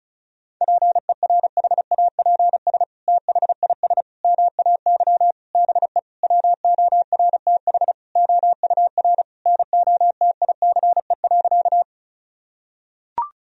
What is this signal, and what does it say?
Morse code, heard as a keyed tone, about 36 wpm, 700 Hz. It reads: PERHAPS THIS MAY BE WORTH OUR NOTICE.
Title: Sentences from Top 1000 Words - Review Part 2 - 35wpm